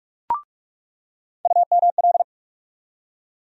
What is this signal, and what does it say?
Morse code, heard as a keyed tone, about 46 wpm, 700 Hz. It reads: UML
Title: USA Call Sign Postfixes - 3 Letters 45wpm